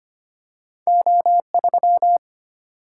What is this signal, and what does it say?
Morse code, heard as a keyed tone, about 25 wpm, 700 Hz. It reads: O3
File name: Two Letter-Number Combinations - 25wpm